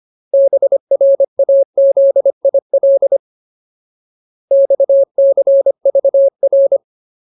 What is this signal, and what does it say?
Morse code, heard as a keyed tone, about 25 wpm, 550 Hz. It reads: BRAZIL XCVR
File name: Ultimate Word List / 25WPM Ultimate